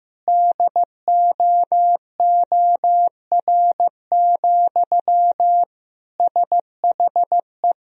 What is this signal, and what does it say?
Morse code, heard as a keyed tone, about 15 wpm, 700 Hz. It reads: DOOR, SHE
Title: Sentences from Top 1000 Words - Review Part 2 - 15wpm